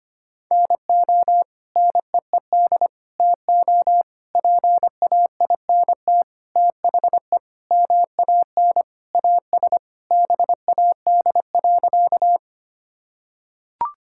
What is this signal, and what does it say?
Morse code, heard as a keyed tone, about 25 wpm, 700 Hz. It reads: NO NEED TO PAINT THE MAN AS BAD.
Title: Sentences from Top 1000 Words - Review Part 6 - 25wpm